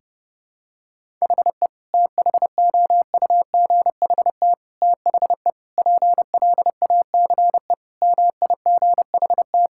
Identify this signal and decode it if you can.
Morse code, heard as a keyed tone, about 30 wpm, 700 Hz. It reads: HE THOUGHT THE PLACE MIGHT
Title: Sentences from Top 1600 Words 30wpm